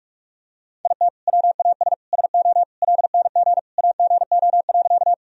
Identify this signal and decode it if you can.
Morse code, heard as a keyed tone, about 46 wpm, 700 Hz. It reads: IT WAS SO LNG AGO.
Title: Sentences from Top 500 Words and Abbreviations 45wpm